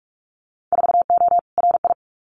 Morse code, heard as a keyed tone, about 44 wpm, 700 Hz. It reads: 4O RI
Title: Winter Field Day Exchanges Encore 45wpm